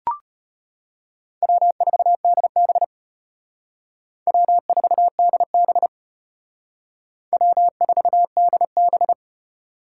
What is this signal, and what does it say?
Morse code, heard as a keyed tone, about 31 wpm, 700 Hz. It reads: W4DB W4DB W4DB